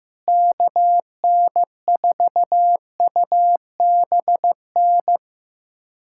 Morse code, heard as a keyed tone, about 15 wpm, 700 Hz. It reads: KN4UBN
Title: USA Callsign DE Callsign - 6 Characters - 15wpm